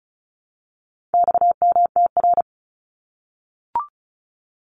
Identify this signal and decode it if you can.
Morse code, heard as a keyed tone, about 34 wpm, 700 Hz. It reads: XMTR
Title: Abbreviations 35wpm